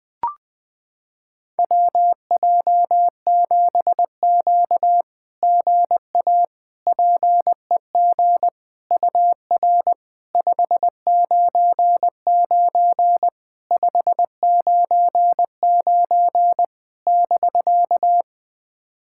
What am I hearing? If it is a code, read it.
Morse code, heard as a keyed tone, about 20 wpm, 690 Hz. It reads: WJ7Q GA PEG UR 599 599 <BK>
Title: SOTA QSO Practice 20wpm